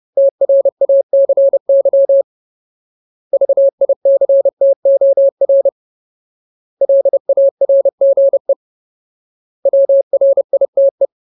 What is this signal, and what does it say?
Morse code, heard as a keyed tone, about 30 wpm, 550 Hz. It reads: TRACY VICTOR LARGE WRITE
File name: Ultimate Word List / 30WPM Ultimate